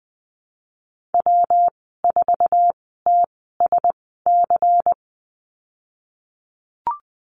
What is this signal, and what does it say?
Morse code, heard as a keyed tone, about 20 wpm, spaced as for 15 wpm, 700 Hz. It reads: W4TSC
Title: 128 Lesson 21g - Number 6 - Call Signs 20wpm